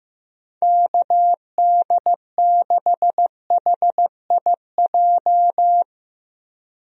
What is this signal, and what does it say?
Morse code, heard as a keyed tone, about 15 wpm, 700 Hz. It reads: KD6HIJ
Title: USA Call Signs - 6 Characters - 15wpm